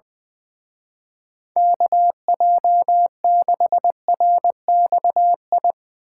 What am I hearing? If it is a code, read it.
Morse code, heard as a keyed tone, about 21 wpm, 700 Hz. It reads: KJ6RXI
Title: Speed Racing - USA 2x3 Call Signs 20wpm